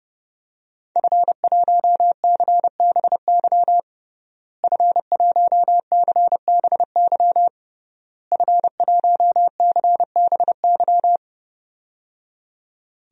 Morse code, heard as a keyed tone, about 30 wpm, 700 Hz. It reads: F1CBY F1CBY F1CBY
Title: DXCC-100 Call Signs - 5 Characters - Repeated 3x - 30wpm